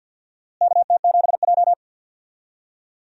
Morse code, heard as a keyed tone, about 50 wpm, 700 Hz. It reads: KT7J
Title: USA Call Signs - 4 Characters - Repeated 2x - 50wpm